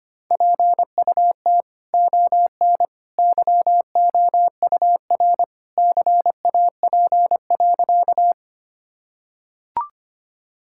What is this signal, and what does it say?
Morse code, heard as a keyed tone, about 25 wpm, 700 Hz. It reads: PUT ON YOUR CAP.
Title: Sentences from Top 1600 Words 25wpm